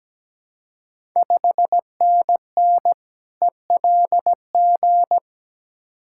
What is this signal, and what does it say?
Morse code, heard as a keyed tone, about 17 wpm, 700 Hz. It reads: 5NN ELG
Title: Ontario QSO Party Exchange 17wpm